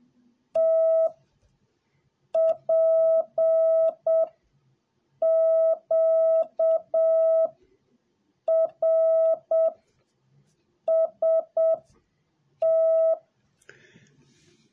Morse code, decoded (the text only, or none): T P Q R S T